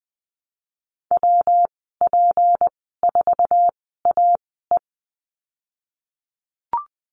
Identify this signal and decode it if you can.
Morse code, heard as a keyed tone, about 19 wpm, spaced as for 14 wpm, 700 Hz. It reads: WP4AE
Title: 193 Lesson 31g - Letter v - Call Signs 20wpm